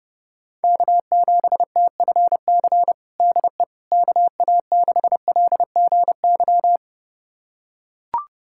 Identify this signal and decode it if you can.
Morse code, heard as a keyed tone, about 30 wpm, 700 Hz. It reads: K7TFC DE KA6LGY
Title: USA Callsign DE Callsign 30wpm